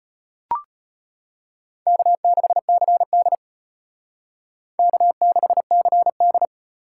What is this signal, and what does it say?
Morse code, heard as a keyed tone, about 32 wpm, 700 Hz. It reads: K6CD K6CD